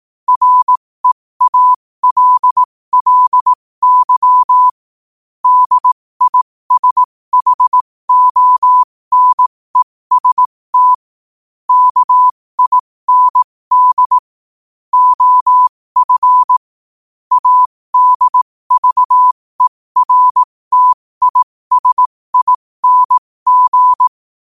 Morse code, heard as a keyed tone, about 18 wpm, spaced as for 15 wpm, 1000 Hz. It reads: REALLY DISHONEST KIND OF ADVERTISING
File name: qotdfiles / 15wpm